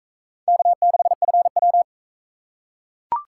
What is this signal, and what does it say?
Morse code, heard as a keyed tone, about 38 wpm, 700 Hz. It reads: K6FW